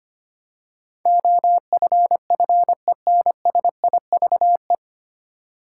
Morse code, heard as a keyed tone, about 25 wpm, 700 Hz. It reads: OFFENSIVE
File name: practice / 25wpm / 9 Letter Words - First 3 Letters Given - 25wpm